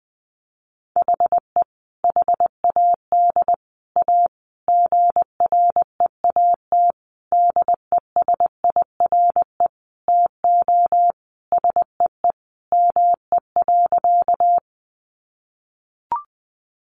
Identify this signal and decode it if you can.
Morse code, heard as a keyed tone, about 19 wpm, 700 Hz. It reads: HE HAD A GREAT DESIRE TO SEE ME.
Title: Sentences from Top 1100 Words 20wpm